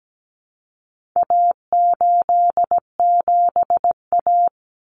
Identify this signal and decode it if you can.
Morse code, heard as a keyed tone, about 17 wpm, 700 Hz. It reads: A87A
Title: Common Cut Numbers - 4-Digit 17wpm